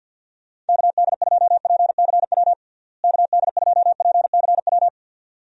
Morse code, heard as a keyed tone, about 52 wpm, 700 Hz. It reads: KD2PCW KD2PCW
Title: USA Call Signs - 6 Characters - Repeated 2x - 50wpm